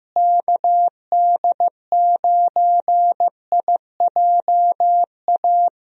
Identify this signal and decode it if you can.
Morse code, heard as a keyed tone, about 15 wpm, 700 Hz. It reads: KD9IJA
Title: USA Call Signs - 6 Characters - Repeated 3x - 15wpm